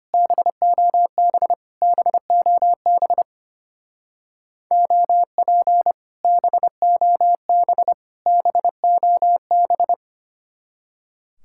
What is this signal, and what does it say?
Morse code, heard as a keyed tone, about 32 wpm, 700 Hz. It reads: BOB BOB OP BOB BOB